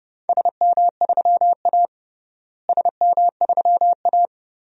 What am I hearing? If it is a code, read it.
Morse code, heard as a keyed tone, about 30 wpm, 700 Hz. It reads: SM3A SM3A
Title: DXCC-100 Call Signs - 4 Characters - Repeated 3x - 30wpm